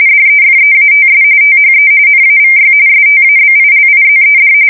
RTTY, about 45.45 baud. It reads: DEFEATS THE NAVY STABILIZED
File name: fsk170 / usnavy4